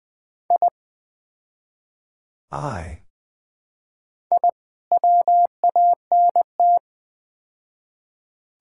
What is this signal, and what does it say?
Morse code, heard as a keyed tone, about 20 wpm, 700 Hz. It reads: I I WANT
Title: Build A Phrase - 4-Word Phrases - 20wpm